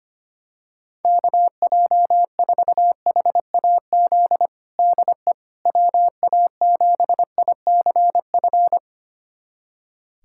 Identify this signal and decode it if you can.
Morse code, heard as a keyed tone, about 25 wpm, 700 Hz. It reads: KJ4HAZ DE WA7ICF
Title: USA Callsign DE Callsign - 6 Characters - 25wpm